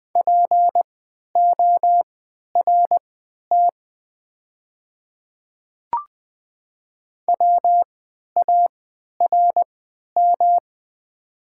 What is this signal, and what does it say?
Morse code, heard as a keyed tone, about 20 wpm, spaced as for 11 wpm, 700 Hz. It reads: PORT WARM
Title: 3x Character Spacing - Top 500 Words - 20wpm